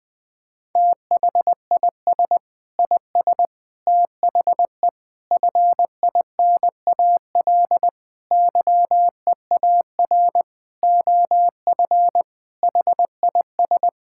Morse code, heard as a keyed tone, about 20 wpm, 700 Hz. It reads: THIS IS THE FINAL YEAR OF HIS
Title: Sentences with 6-Word Phrases - 20wpm